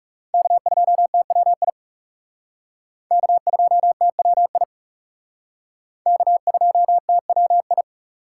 Morse code, heard as a keyed tone, about 37 wpm, 700 Hz. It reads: K2TWI K2TWI K2TWI
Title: Speed Racing - USA 1x3 Call Signs 30wpm